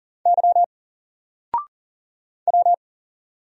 Morse code, heard as a keyed tone, about 40 wpm, 700 Hz. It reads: Y W
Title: Single Letter-Number 40wpm